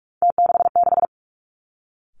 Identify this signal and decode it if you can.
Morse code, heard as a keyed tone, about 46 wpm, 700 Hz. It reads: T66
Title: Three Challenging Letter-Number Combinations 45wpm